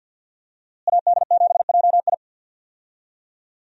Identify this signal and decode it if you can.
Morse code, heard as a keyed tone, about 46 wpm, 700 Hz. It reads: AD7JI
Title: USA Call Signs - 5 Characters - Repeated 2x - 50wpm